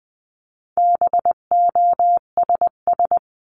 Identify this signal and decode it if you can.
Morse code, heard as a keyed tone, about 20 wpm, 700 Hz. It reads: BOSS
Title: 259 Lesson 42d - Words 18-20wpm